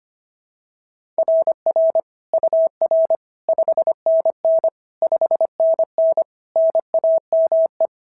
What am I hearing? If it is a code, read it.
Morse code, heard as a keyed tone, about 24 wpm, 650 Hz. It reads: RR UR 5NN 5NN NAME
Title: SOTA QSO Practice 25wpm